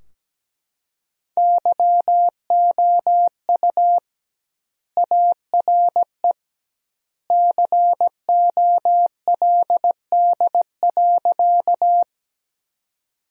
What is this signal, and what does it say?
Morse code, heard as a keyed tone, about 17 wpm, 700 Hz. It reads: YOU ARE COLD.
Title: 2x Word-Spacing - Sentences from Top 400 Words - 17wpm